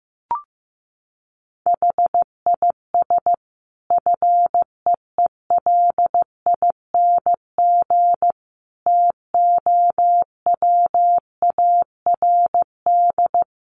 Morse code, heard as a keyed tone, about 15 wpm, 700 Hz. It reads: HIS FEELING TOWARD
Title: Sentences from Top 1300 Words 15wpm